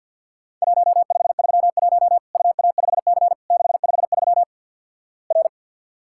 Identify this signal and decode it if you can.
Morse code, heard as a keyed tone, about 50 wpm, 700 Hz. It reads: 1531 UA5C 653 R
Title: Worked All Europe DX Contest - QTC Exchange with Repeats - 50wpm